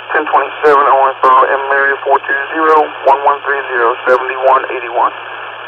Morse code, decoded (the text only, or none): none